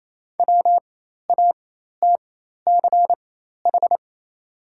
Morse code, heard as a keyed tone, about 28 wpm, spaced as for 13 wpm, 700 Hz. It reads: WATCH
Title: 4x Character Spacing - Top 500 Words - 28wpm